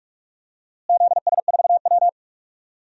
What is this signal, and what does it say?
Morse code, heard as a keyed tone, about 45 wpm, 700 Hz. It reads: ZS4W
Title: DXCC-100 Call Signs - 4 Characters - Repeated 2x - 45wpm